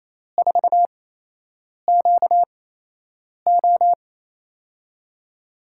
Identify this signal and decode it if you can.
Morse code, heard as a keyed tone, about 28 wpm, spaced as for 8 wpm, 700 Hz. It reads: 4QO
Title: USA Call Signs - 5 Characters - 8x Character Spacing - 28wpm